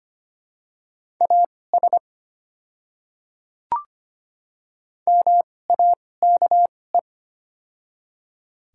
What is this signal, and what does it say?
Morse code, heard as a keyed tone, about 25 wpm, spaced as for 18 wpm, 700 Hz. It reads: AS MAKE
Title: 2x Character Spacing - Top 300 Words - 25wpm